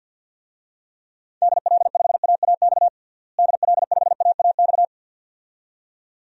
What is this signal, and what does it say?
Morse code, heard as a keyed tone, about 51 wpm, 700 Hz. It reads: DL5AAX DL5AAX
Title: DXCC-100 Call Signs - 6 Characters - Repeated 2x - 50wpm